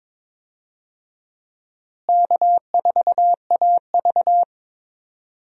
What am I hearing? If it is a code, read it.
Morse code, heard as a keyed tone, about 23 wpm, 700 Hz. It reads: K4AV
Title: USA Call Signs - 4 Characters - Repeated 3x - 22wpm